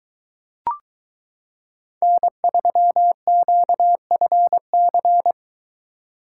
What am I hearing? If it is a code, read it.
Morse code, heard as a keyed tone, about 22 wpm, 700 Hz. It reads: N3QFC